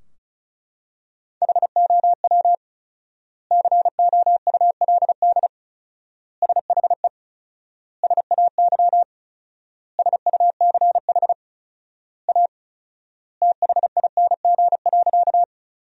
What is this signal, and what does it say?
Morse code, heard as a keyed tone, about 36 wpm, 700 Hz. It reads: HOW COULD SHE SAY SUCH A THING.
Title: 4x Word-Spacing - Sentences from Top 200 Words - 35wpm